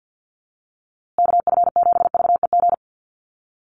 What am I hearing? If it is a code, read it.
Morse code, heard as a keyed tone, about 48 wpm, 700 Hz. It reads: KF7VEG
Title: USA Call Signs Repeated 2x 50wpm